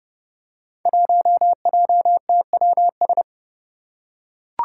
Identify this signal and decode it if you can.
Morse code, heard as a keyed tone, about 30 wpm, 700 Hz. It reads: 1JTWS
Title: Five Challenging Letter-Number Combinations 30wpm